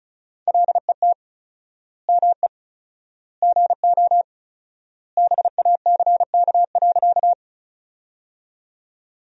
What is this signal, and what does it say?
Morse code, heard as a keyed tone, about 34 wpm, 700 Hz. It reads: LET ME GO BACK.
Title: 4x Word-Spacing - Sentences from Top 300 Words - 35wpm